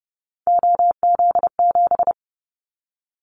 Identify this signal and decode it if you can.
Morse code, heard as a keyed tone, about 30 wpm, 700 Hz. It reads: OZ7
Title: Callsign Prefixes for Top 100 DXCC - 30wpm